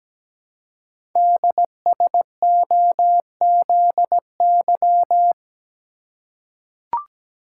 Morse code, heard as a keyed tone, about 17 wpm, 700 Hz. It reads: DSOZY
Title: Code Groups - 5 Letters - 17wpm